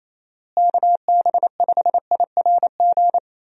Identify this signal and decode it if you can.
Morse code, heard as a keyed tone, about 27 wpm, 700 Hz. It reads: KB5IRG
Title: USA Call Signs - 6 Characters - Repeated 3x - 28wpm